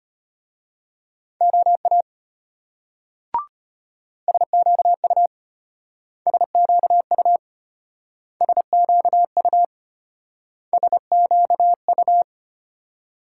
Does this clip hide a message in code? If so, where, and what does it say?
Yes: Morse code from 1.4 to 12.2 s, heard as a keyed tone, about 33 wpm, 700 Hz. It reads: OA SQU SQU SQU SQU